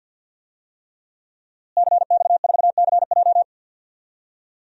Morse code, heard as a keyed tone, about 49 wpm, 700 Hz. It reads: CX4CJ
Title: DXCC-100 Call Signs - 5 Characters - Repeated 3x - 50wpm